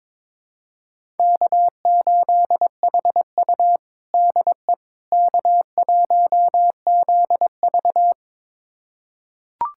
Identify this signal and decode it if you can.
Morse code, heard as a keyed tone, about 21 wpm, 700 Hz. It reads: K8HU DE K1ZV
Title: USA Callsign DE Callsign - 4 Characters - 22wpm